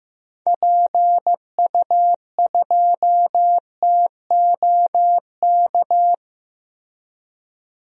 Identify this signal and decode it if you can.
Morse code, heard as a keyed tone, about 15 wpm, 700 Hz. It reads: PU2TOK
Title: DXCC-100 Call Signs - 6 Characters - Repeated 3x - 15wpm